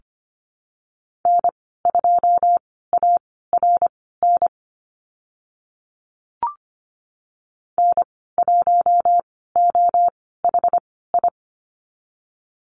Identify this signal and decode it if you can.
Morse code, heard as a keyed tone, about 25 wpm, spaced as for 16 wpm, 690 Hz. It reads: N2ARN N1OHI
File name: Lesson 15g - Number 5 - Call Signs 25wpm